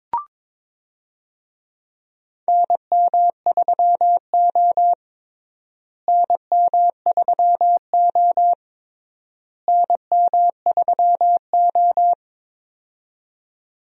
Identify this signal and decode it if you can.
Morse code, heard as a keyed tone, about 22 wpm, 700 Hz. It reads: NM3O NM3O NM3O
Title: USA Call Signs - 4 Characters - Repeated 3x - 22wpm